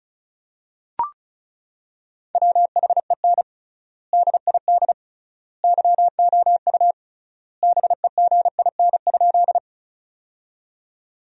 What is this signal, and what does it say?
Morse code, heard as a keyed tone, about 36 wpm, 700 Hz. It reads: WHEN DID YOU BEGIN?
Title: 3x Word-Spacing - Sentences from Top 300 Words - 35wpm